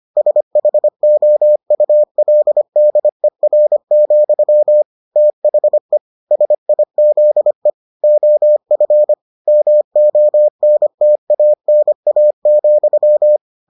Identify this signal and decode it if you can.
Morse code, heard as a keyed tone, about 25 wpm, 600 Hz. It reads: SHOULDER, THE SIZE OF MONTANA,